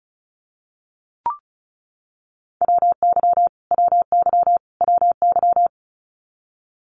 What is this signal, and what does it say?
Morse code, heard as a keyed tone, about 34 wpm, 700 Hz. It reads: WY WY WY
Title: US States Repeated 3x 35wpm